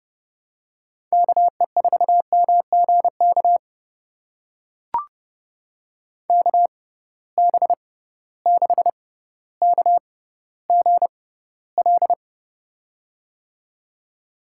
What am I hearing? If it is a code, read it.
Morse code, heard as a keyed tone, about 30 wpm, spaced as for 15 wpm, 700 Hz. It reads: KE4MGK KB6KGL
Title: USA Call Signs - 6 Characters - 6x Character Spacing - 30wpm